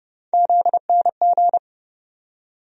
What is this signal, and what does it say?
Morse code, heard as a keyed tone, about 31 wpm, 700 Hz. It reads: ZNG
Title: USA Call Sign Postfixes - 3 Letters 30wpm